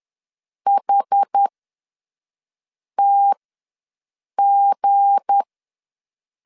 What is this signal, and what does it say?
Morse code, heard as a keyed tone, about 10 wpm, spaced as for 5 wpm, 800 Hz. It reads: HTG
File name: LEZIONI / 7 GF9Z / 7 G ws 10wpm 3843